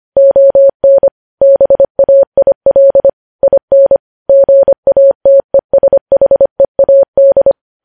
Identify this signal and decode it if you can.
Morse code, heard as a keyed tone, about 25 wpm, 550 Hz. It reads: ON BAIL IN GATESHEAD
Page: News Headlines in Morse Code at 25 WPM